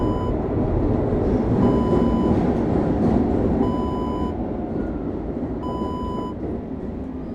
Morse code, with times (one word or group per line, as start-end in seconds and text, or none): none